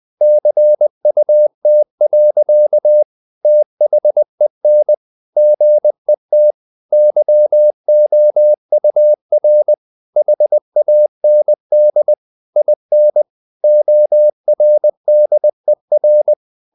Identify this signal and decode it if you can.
Morse code, heard as a keyed tone, about 20 wpm, 600 Hz. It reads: CUT. THEN GET YOUR HAND IN ORDER
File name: the old man and the sea part 2 20wpm